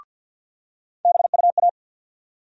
Morse code, heard as a keyed tone, about 46 wpm, 700 Hz. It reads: BUA
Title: USA Call Sign Postfixes - 3 Letters 50wpm